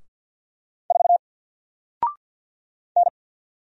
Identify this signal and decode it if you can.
Morse code, heard as a keyed tone, about 48 wpm, 700 Hz. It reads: 4 N
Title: Single Character (Letter:Number:Punctuation:Prosign) Alt 50wpm